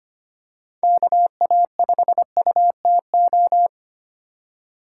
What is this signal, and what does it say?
Morse code, heard as a keyed tone, about 25 wpm, 700 Hz. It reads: KA5UTO
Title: USA Call Signs - 3x Character Spacing - 25wpm